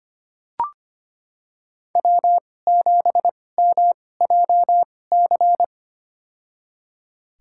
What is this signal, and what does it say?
Morse code, heard as a keyed tone, about 25 wpm, spaced as for 18 wpm, 700 Hz. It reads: W7MJC
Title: USA Call Signs - 2x Character Spacing - 25wpm